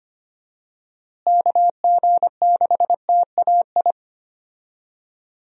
Morse code, heard as a keyed tone, about 25 wpm, 700 Hz. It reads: KG6TAI
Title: USA Call Signs - 6 Characters - Repeated 2x - 25wpm